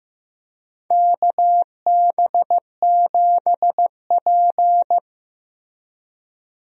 Morse code, heard as a keyed tone, about 16 wpm, 700 Hz. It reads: KB7P
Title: USA Call Signs - 4 Characters - Repeated 3x - 15wpm